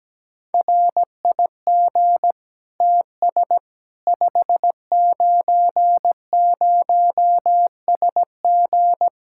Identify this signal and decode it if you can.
Morse code, heard as a keyed tone, about 17 wpm, 700 Hz. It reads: RIG TS 590SG
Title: Practice QSO - Exchange 2 - Info - 17wpm